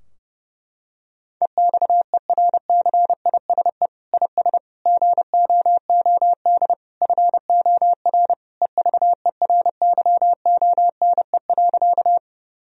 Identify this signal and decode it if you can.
Morse code, heard as a keyed tone, about 30 wpm, 700 Hz. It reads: EXERCISE IS GOOD FOR EVERYONE.
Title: Sentences from Top 1000 Words - Review Part 4 - 30wpm